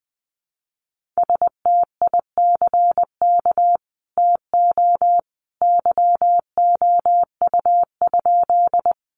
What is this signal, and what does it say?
Morse code, heard as a keyed tone, about 20 wpm, 700 Hz. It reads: STICK TO YOU?
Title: Sentences from Top 1200 Words 20wpm